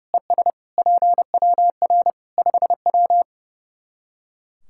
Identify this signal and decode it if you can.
Morse code, heard as a keyed tone, about 30 wpm, 700 Hz. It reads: ES PWR 5W
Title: Practice QSO - Exchange 2 - Info - 30wpm